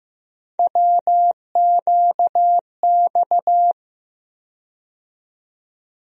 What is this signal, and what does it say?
Morse code, heard as a keyed tone, about 15 wpm, 700 Hz. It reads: WQX
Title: USA Call Sign Postfixes - 3 Letters 15wpm